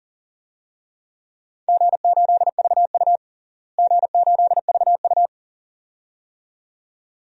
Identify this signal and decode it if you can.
Morse code, heard as a keyed tone, about 38 wpm, 700 Hz. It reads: G8VU G8VU
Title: DXCC-100 Call Signs - 4 Characters - Repeated 2x - 40wpm